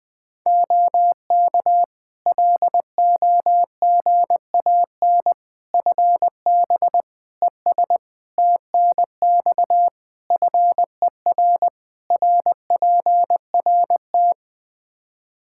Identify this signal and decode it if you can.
Morse code, heard as a keyed tone, about 20 wpm, 700 Hz. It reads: OK LOGAN FB ES TNX FER RPRT
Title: Practice QSO - Exchange 2 - Info - 20wpm